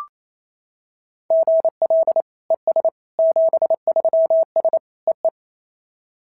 Morse code, heard as a keyed tone, about 28 wpm, 690 Hz. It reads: GL ES 73S EE
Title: SOTA QSO Practice 28wpm